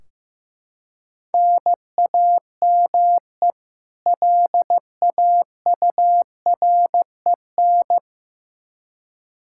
Morse code, heard as a keyed tone, about 15 wpm, 700 Hz. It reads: NAME LAUREN
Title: K1USN SST QSO Practice 15wpm